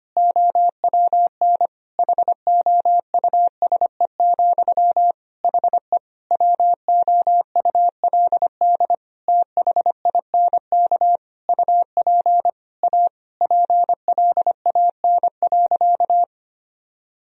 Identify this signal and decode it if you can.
Morse code, heard as a keyed tone, about 25 wpm, 700 Hz. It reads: OWN HOUSE, HE WOULD THINK UP A PLAN.
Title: Sentences from Top 1000 Words - Review Part 2 - 25wpm